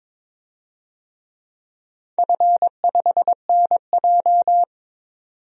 Morse code, heard as a keyed tone, about 22 wpm, 700 Hz. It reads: F5NJ
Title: DXCC-100 Call Signs - 4 Characters - 22wpm